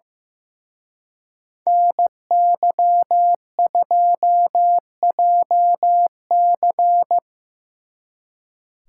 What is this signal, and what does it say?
Morse code, heard as a keyed tone, about 18 wpm, 700 Hz. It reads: NY2JC
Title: Speed Racing - USA Call Signs 15wpm